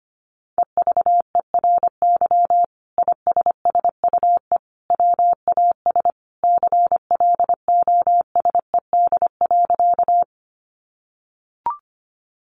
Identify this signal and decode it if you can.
Morse code, heard as a keyed tone, about 25 wpm, 700 Hz. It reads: EVERY ISSUE WAS CLOSED.
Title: Sentences from Top 1400 Words 25wpm